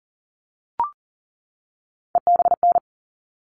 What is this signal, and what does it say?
Morse code, heard as a keyed tone, about 40 wpm, 700 Hz. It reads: EBN
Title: All Cut Numbers - 3-Digit 40wpm